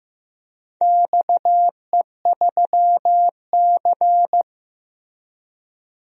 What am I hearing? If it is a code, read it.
Morse code, heard as a keyed tone, about 16 wpm, 700 Hz. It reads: XE3C
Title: DXCC-100 Call Signs - 4 Characters - Repeated 3x - 15wpm